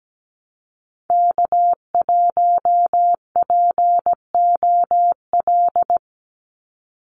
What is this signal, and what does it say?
Morse code, heard as a keyed tone, about 17 wpm, 700 Hz. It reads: K1POL
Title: USA Call Signs 17wpm